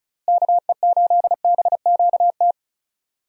Morse code, heard as a keyed tone, about 36 wpm, 700 Hz. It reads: KE8BQT
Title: USA Call Signs - 6 Characters - Repeated 2x - 35wpm